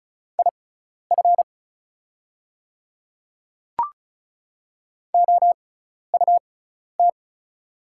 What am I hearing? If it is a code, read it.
Morse code, heard as a keyed tone, about 36 wpm, spaced as for 12 wpm, 690 Hz. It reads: IF OUT
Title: 6x Character Spacing - Top 100 Words - 35wpm